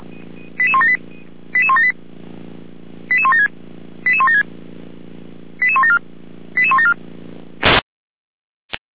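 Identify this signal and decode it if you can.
Five-tone selective call (ZVEI), heard as quick runs of five tones, tones of about 70 ms. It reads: 88178 88178 88176 88176 88174 88174
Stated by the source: ZVEI five-tone selective call received via SDR.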